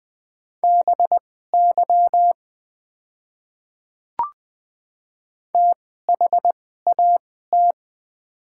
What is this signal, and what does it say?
Morse code, heard as a keyed tone, about 20 wpm, spaced as for 14 wpm, 700 Hz. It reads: BY THAT
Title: 2x Character Spacing - Top 100 Words - 20wpm